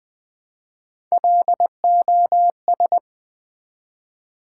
Morse code, heard as a keyed tone, about 20 wpm, 700 Hz. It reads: LOS
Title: Build A Word - 1 Letter At A Time - 3 to 5-Letter Words - 20wpm